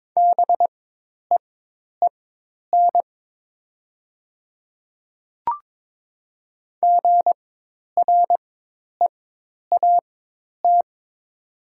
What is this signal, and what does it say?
Morse code, heard as a keyed tone, about 22 wpm, spaced as for 10 wpm, 700 Hz. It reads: BEEN GREAT
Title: 4x Character Spacing - Top 200 Words - 22wpm